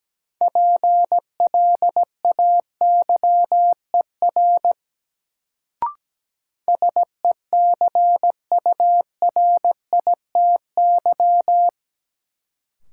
Morse code, heard as a keyed tone, about 17 wpm, 700 Hz. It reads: PLAYER SECURITY
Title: Sentences from Top 1000 Words - Review Part 4 - New Word Drill - 17wpm